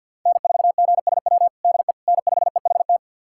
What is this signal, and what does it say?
Morse code, heard as a keyed tone, about 49 wpm, 700 Hz. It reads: N4GSW DE N5EHT
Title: USA Callsign DE Callsign - 5 Characters - 50wpm